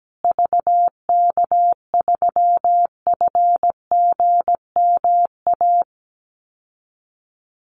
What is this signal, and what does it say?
Morse code, heard as a keyed tone, about 17 wpm, 700 Hz. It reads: VK3FGMA
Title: USA Call Signs Repeated 2x 17wpm